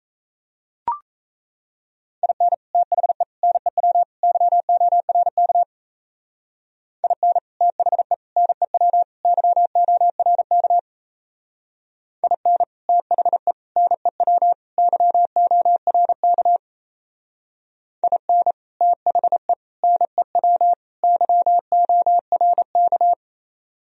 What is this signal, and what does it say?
Morse code, heard as a keyed tone, about 34 wpm, 700 Hz. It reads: IN THE NEW YORK IN THE NEW YORK IN THE NEW YORK IN THE NEW YORK
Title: Speed-Racing - Sets of 4 Words 28wpm